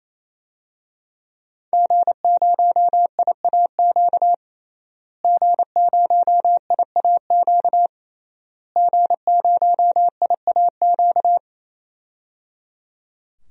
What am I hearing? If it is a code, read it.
Morse code, heard as a keyed tone, about 28 wpm, 700 Hz. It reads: G0IAQ G0IAQ G0IAQ
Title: DXCC-100 Call Signs - 5 Characters - Repeated 3x - 28wpm